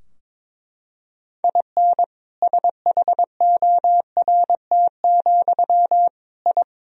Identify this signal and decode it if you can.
Morse code, heard as a keyed tone, about 22 wpm, 700 Hz. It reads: IN SHORT, I
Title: Sentences from Top 1000 Words - Review Part 2 - 22wpm